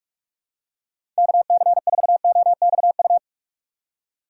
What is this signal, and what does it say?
Morse code, heard as a keyed tone, about 45 wpm, 700 Hz. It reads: KC4OXU
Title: USA Call Signs - 6 Characters - 8x Character Spacing - 45wpm